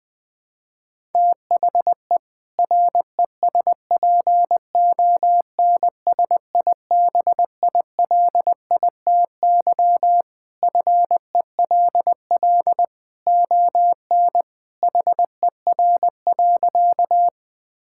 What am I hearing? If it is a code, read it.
Morse code, heard as a keyed tone, about 20 wpm, 700 Hz. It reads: THE RESPONSIBILITY FELL ON HER.
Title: Sentences with 2-Word Phrases - 20wpm